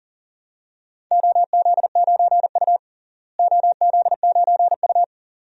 Morse code, heard as a keyed tone, about 40 wpm, 700 Hz. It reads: OZ9U OZ9U
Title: DXCC-100 Call Signs - 4 Characters - Repeated 2x - 40wpm